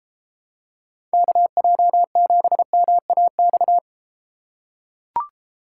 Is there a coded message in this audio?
Yes: Morse code from 1.1 to 3.8 s, heard as a keyed tone, about 30 wpm, 700 Hz. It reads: KJ7MAX